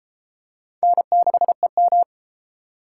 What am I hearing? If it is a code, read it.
Morse code, heard as a keyed tone, about 31 wpm, 700 Hz. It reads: N6EM